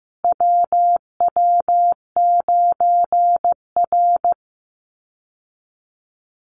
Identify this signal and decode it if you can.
Morse code, heard as a keyed tone, about 15 wpm, 700 Hz. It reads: WW9R
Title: Call Signs Repeated 3x - 15wpm